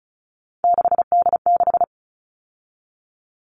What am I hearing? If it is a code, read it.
Morse code, heard as a keyed tone, about 34 wpm, 700 Hz. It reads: 6D6
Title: All Cut Numbers - 3-Digit 35wpm